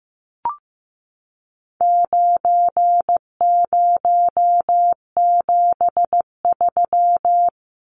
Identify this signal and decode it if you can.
Morse code, heard as a keyed tone, about 15 wpm, 700 Hz. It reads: 9073
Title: Numbers - 4-Digit 15wpm